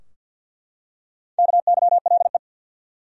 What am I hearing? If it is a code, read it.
Morse code, heard as a keyed tone, about 47 wpm, 700 Hz. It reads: KYLE
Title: USA Common Names - Max 5 Letters - 50wpm